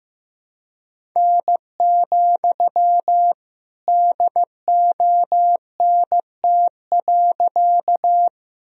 Morse code, heard as a keyed tone, about 15 wpm, 700 Hz. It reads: N, DONT.
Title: Sentences from Top 300 Words and Abbreviations 15wpm